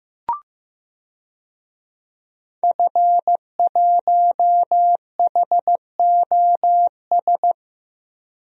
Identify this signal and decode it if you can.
Morse code, heard as a keyed tone, about 15 wpm, 700 Hz. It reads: F1HOS
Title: DXCC-100 Call Signs - 5 Characters - Repeated 2x - 15wpm